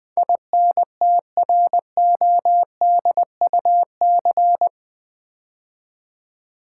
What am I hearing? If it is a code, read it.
Morse code, heard as a keyed tone, about 20 wpm, 700 Hz. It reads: INTRODUC
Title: Build A Word - 4 Letters At A Time - 12-Letter Words - 20wpm